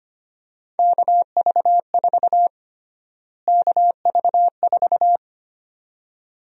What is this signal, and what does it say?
Morse code, heard as a keyed tone, about 25 wpm, 700 Hz. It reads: KV4 KV4
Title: USA Call Sign Prefixes - Repeated 2x - 25wpm